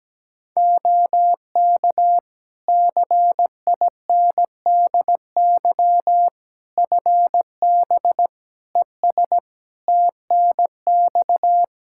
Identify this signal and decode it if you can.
Morse code, heard as a keyed tone, about 17 wpm, 700 Hz. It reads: OK CINDY FB ES TNX
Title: K1USN SST QSO Practice 17wpm